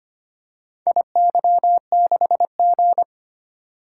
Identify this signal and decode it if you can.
Morse code, heard as a keyed tone, about 25 wpm, 700 Hz. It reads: IY6G
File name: practice / 25wpm / Four Letter-Number Combinations - 25wpm